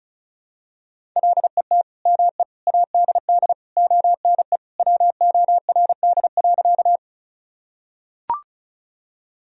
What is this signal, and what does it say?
Morse code, heard as a keyed tone, about 34 wpm, 700 Hz. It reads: LET ME ADD ONE WORD.
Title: Sentences from Top 1000 Words - Review Part 1 - 35wpm